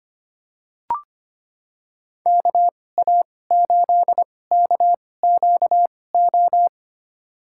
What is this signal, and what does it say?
Morse code, heard as a keyed tone, about 25 wpm, spaced as for 18 wpm, 700 Hz. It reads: KA8KQO
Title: USA Call Signs - 2x Character Spacing - 25wpm